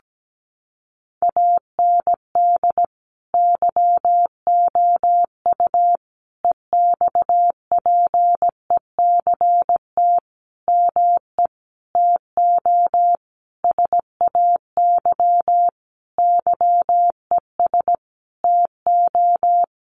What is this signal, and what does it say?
Morse code, heard as a keyed tone, about 17 wpm, 700 Hz. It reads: AND YOU EXPECT ME TO SAY YES TO
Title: Sentences from Top 1100 Words 17wpm